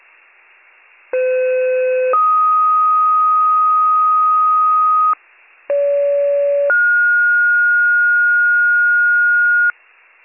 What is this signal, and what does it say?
Two-tone page, heard as a one-second tone, then a three-second tone, about 510 then 1250 Hz, about 570 then 1430 Hz. It is heard from 1.1 to 9.7 s.